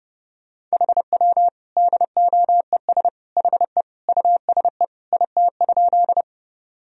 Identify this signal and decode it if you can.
Morse code, heard as a keyed tone, about 32 wpm, 700 Hz. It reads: HW DOES HE USE IT?
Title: Sentences from Top 200 Words and Abbreviations 30wpm